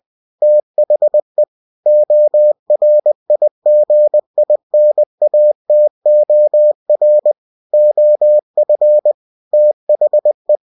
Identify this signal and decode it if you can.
Morse code, heard as a keyed tone, about 20 wpm, 600 Hz. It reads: THE ORIGINATOR OF THE
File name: cw-books / winnie-the-pooh / winnie-20wpm-page-0122